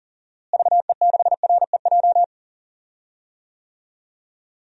Morse code, heard as a keyed tone, about 40 wpm, 700 Hz. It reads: VE6REJ
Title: DXCC-100 Call Signs - 6 Characters - Repeated 3x - 40wpm